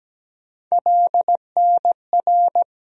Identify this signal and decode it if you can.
Morse code, heard as a keyed tone, about 17 wpm, 700 Hz. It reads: LNR
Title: USA Call Sign Postfixes - 3 Letters 17wpm